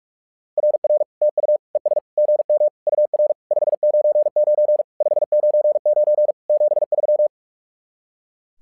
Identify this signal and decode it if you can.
Morse code, heard as a keyed tone, about 44 wpm, 600 Hz. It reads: RR TU ES GM UR 599 599 73
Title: SOTA QSO Practice 45wpm